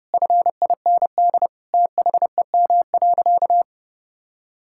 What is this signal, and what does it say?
Morse code, heard as a keyed tone, about 30 wpm, 700 Hz. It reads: FIND THEM.
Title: Sentences from Top 1000 Words - Review Part 6 - 30wpm